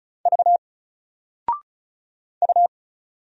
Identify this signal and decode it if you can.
Morse code, heard as a keyed tone, about 35 wpm, 700 Hz. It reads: V U
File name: Single-Letter-Number-35wpm